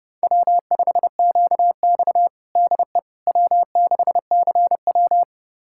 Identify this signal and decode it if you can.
Morse code, heard as a keyed tone, about 32 wpm, 700 Hz. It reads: W5QX DE W6CW
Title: USA Callsign DE Callsign - 4 Characters - 30wpm